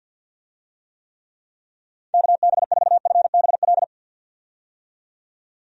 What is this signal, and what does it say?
Morse code, heard as a keyed tone, about 47 wpm, 700 Hz. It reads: KB4FBL
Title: USA Call Signs - 6 Characters - 50wpm